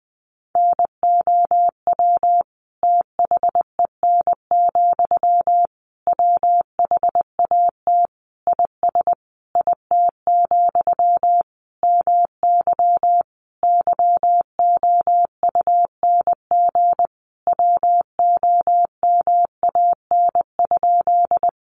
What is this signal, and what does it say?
Morse code, heard as a keyed tone, about 19 wpm, 700 Hz. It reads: NOW THEN, WHAT IS IT, MY YOUNG WOMAN?
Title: Sentences from Top 1100 Words 20wpm